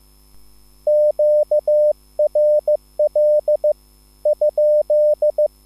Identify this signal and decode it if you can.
Morse code, heard as a keyed tone, about 15 wpm, spaced as for 13 wpm, 600 Hz. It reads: QRL?